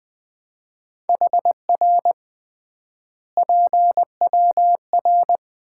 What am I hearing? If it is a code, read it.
Morse code, heard as a keyed tone, about 20 wpm, 700 Hz. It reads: HR PWR
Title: Common QSO Phrases - 3x Word-Spacing - 20wpm